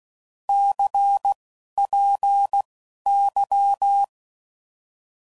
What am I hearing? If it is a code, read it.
Morse code, heard as a keyed tone, about 16 wpm, spaced as for 11 wpm, 800 Hz. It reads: CPY
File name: Mixed / Abreviations / Abreviations